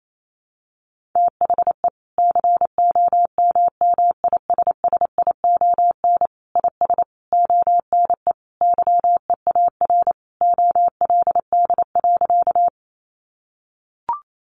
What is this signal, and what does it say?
Morse code, heard as a keyed tone, about 28 wpm, 700 Hz. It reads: THE COMMISSION IS ONE YEAR OLD.
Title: Sentences from Top 1400 Words 28wpm